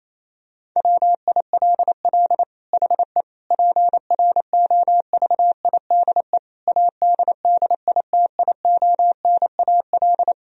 Morse code, heard as a keyed tone, about 27 wpm, 700 Hz. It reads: WILL HE PROVIDE ADDITIONAL
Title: Sentences from Top 1000 Words - Review Part 4 - 28wpm